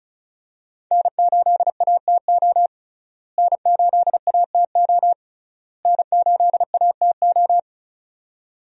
Morse code, heard as a keyed tone, about 36 wpm, 700 Hz. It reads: N8ATO N8ATO N8ATO
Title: USA Call Signs - 5 Characters - Repeated 3x - 35wpm